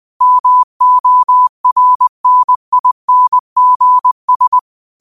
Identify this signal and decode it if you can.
Morse code, heard as a keyed tone, about 20 wpm, 1000 Hz. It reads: MORNINGS
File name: 20wpm